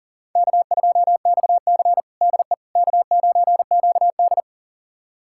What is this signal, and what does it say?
Morse code, heard as a keyed tone, about 37 wpm, 700 Hz. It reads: K2XC DE K9QD
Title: USA Callsign DE Callsign - 4 Characters - 40wpm